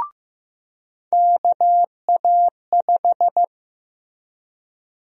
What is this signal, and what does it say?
Morse code, heard as a keyed tone, about 15 wpm, 700 Hz. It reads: KA5
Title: USA Call Sign Prefixes - Repeated 2x - 15wpm